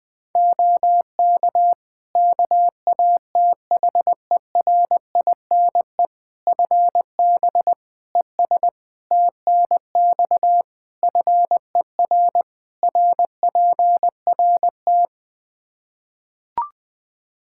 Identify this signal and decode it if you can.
Morse code, heard as a keyed tone, about 20 wpm, 700 Hz. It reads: OK KATHERINE FB ES TNX FER RPRT
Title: K1USN SST QSO Practice 20wpm